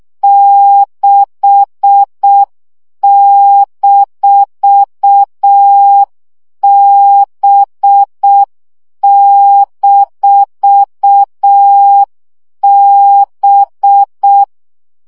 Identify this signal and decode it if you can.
Morse code, heard as a keyed tone, about 6 wpm, 800 Hz. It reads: 6-B-B
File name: CW gyakorlas b 6 egyenlo minusz 5 baud